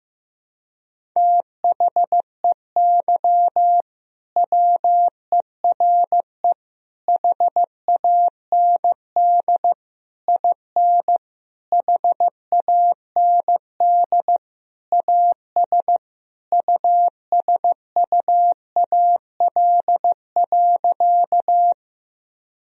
Morse code, heard as a keyed tone, about 15 wpm, 700 Hz. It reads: THEY WERE HAND IN HAND AS USUAL.
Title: Sentences from Top 1000 Words - Review Part 2 - 15wpm